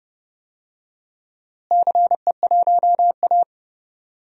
Morse code, heard as a keyed tone, about 30 wpm, 700 Hz. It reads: CE1A
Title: DXCC-100 Call Signs - 4 Characters - Repeated 2x - 30wpm